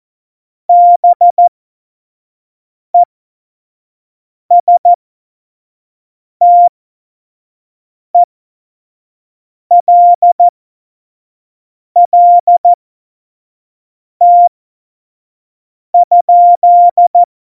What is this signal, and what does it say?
Morse code, heard as a keyed tone, about 14 wpm, spaced as for 5 wpm, 700 Hz. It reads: BESTELLT?
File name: Morsekurs / Witze / Witz 15. Oktober 2019, 25 F70